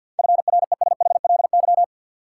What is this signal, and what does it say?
Morse code, heard as a keyed tone, about 50 wpm, 700 Hz. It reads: FRESHLY
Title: Sets of 5 Words Encore - New Word Drill - 50wpm